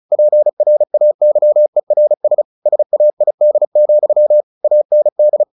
Morse code, heard as a keyed tone, about 34 wpm, 600 Hz. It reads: PRAYERS SAID, AND
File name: the old man and the sea / the old man and the sea 1 35wpm